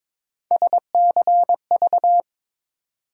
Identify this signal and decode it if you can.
Morse code, heard as a keyed tone, about 22 wpm, 700 Hz. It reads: SCV
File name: ARRL Sections v2 - 22wpm